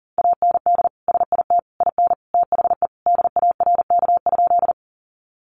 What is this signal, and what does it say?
Morse code, heard as a keyed tone, about 40 wpm, 700 Hz. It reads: AND SIT IN THE DARK?
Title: Sentences from Top 500 Words 40wpm